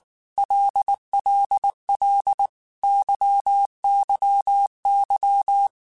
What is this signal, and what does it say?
Morse code, heard as a keyed tone, about 19 wpm, 800 Hz. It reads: LLL YYY
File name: Groups@18 / Groups@18 mp3 / LFCYQ@18